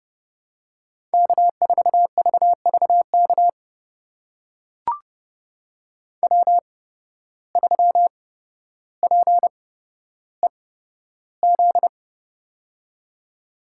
Morse code, heard as a keyed tone, about 30 wpm, spaced as for 13 wpm, 700 Hz. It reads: K4VVK W3PEZ
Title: USA Call Signs - 5 Characters - 8x Character Spacing - 30wpm